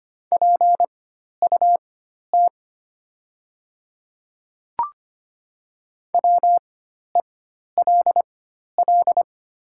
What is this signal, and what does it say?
Morse code, heard as a keyed tone, about 25 wpm, spaced as for 12 wpm, 700 Hz. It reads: PUT WELL
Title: 4x Character Spacing - Top 200 Words - 25wpm